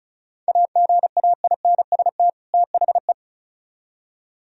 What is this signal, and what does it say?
Morse code, heard as a keyed tone, about 45 wpm, 700 Hz. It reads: AGAINST THE